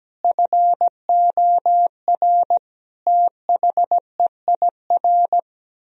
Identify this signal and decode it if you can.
Morse code, heard as a keyed tone, about 17 wpm, 700 Hz. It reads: FOR THEIR
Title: Sentences from Top 1000 Words - Review Part 4 - 17wpm